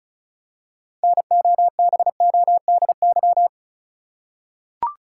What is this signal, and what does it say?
Morse code, heard as a keyed tone, about 34 wpm, 700 Hz. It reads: NOBODY
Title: Six Letter Words - 35wpm